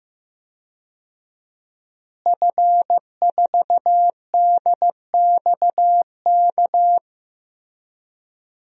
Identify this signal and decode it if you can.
Morse code, heard as a keyed tone, about 15 wpm, 700 Hz. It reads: F4DXK
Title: DXCC-100 Call Signs - 5 Characters - Repeated 2x - 15wpm